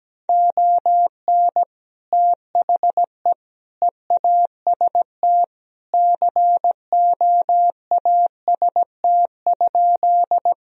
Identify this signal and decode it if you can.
Morse code, heard as a keyed tone, about 17 wpm, 700 Hz. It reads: ON THE EAST COAST?
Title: Sentences from Top 1000 Words - Review Part 4 - 17wpm